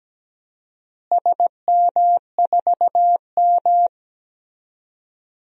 Morse code, heard as a keyed tone, about 17 wpm, 700 Hz. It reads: SM4M
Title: DXCC-100 Call Signs - 4 Characters - Repeated 3x - 17wpm